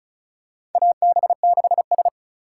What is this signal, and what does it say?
Morse code, heard as a keyed tone, about 34 wpm, 700 Hz. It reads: AB6S
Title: USA Call Signs - 4 Characters - 4x Character Spacing - 35wpm